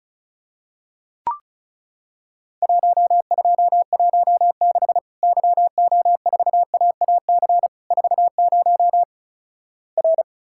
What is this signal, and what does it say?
Morse code, heard as a keyed tone, about 36 wpm, 700 Hz. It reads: 1216 YO4AAC 40 R
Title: Worked All Europe DX Contest - QTC Exchange - 35wpm